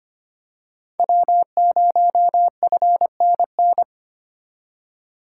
Morse code, heard as a keyed tone, about 25 wpm, 700 Hz. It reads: W0FNN
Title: USA Call Signs - 5 Characters - 4x Character Spacing - 25wpm